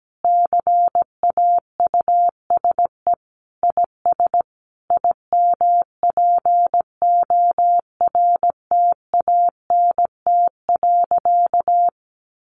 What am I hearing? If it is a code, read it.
Morse code, heard as a keyed tone, about 17 wpm, 700 Hz. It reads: CAUSE IS IMPORTANT.
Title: Sentences from Top 1200 Words 17wpm